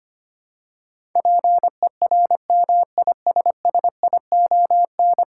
Morse code, heard as a keyed tone, about 25 wpm, 700 Hz. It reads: PERMISSION
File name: 10 Letter Words - First 3 Letters Given - 25wpm